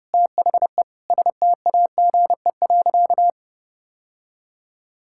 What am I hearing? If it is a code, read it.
Morse code, heard as a keyed tone, about 32 wpm, 700 Hz. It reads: THE STAGE.
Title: Sentences from Top 1600 Words 30wpm